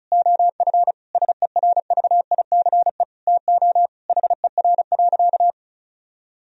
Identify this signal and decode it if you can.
Morse code, heard as a keyed tone, about 36 wpm, 700 Hz. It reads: OF SERVICE TO HER.
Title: Sentences from Top 1000 Words - Review Part 4 - 35wpm